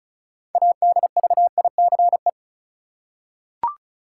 Morse code, heard as a keyed tone, about 36 wpm, 700 Hz. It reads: ADVICE
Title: Six Letter Words - 35wpm